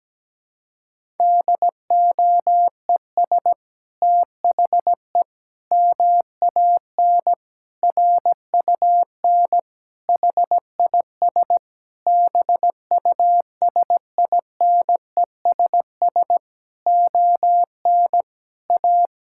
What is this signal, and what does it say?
Morse code, heard as a keyed tone, about 17 wpm, 700 Hz. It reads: DOES THE MAN RUN HIS BUSINESS ON A
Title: Sentences from Top 1000 Words - Review Part 5 - 17wpm